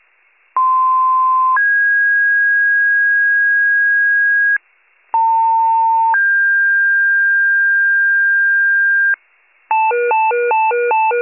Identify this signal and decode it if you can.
Two-tone page, heard as a one-second tone, then a three-second tone, about 1040 then 1650 Hz, about 900 then 1600 Hz. It is heard from 0.6 to 9.2 s.